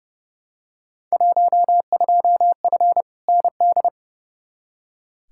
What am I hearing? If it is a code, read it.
Morse code, heard as a keyed tone, about 31 wpm, 700 Hz. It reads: 12F ND
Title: Field Day Exchanges Encore - 30wpm